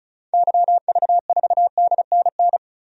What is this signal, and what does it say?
Morse code, heard as a keyed tone, about 34 wpm, 700 Hz. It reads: YV4DNN
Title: DXCC-100 Call Signs - 6 Characters - 35wpm